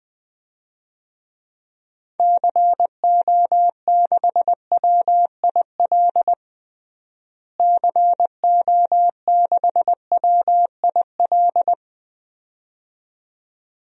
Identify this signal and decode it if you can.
Morse code, heard as a keyed tone, about 20 wpm, 700 Hz. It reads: CO6WIL CO6WIL
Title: DXCC-100 Call Signs - 6 Characters - Repeated 2x - 20wpm